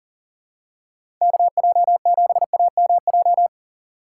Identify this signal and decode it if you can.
Morse code, heard as a keyed tone, about 39 wpm, 700 Hz. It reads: KJ7AMJ
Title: USA Call Signs - 6 Characters - 3x Character Spacing - 40wpm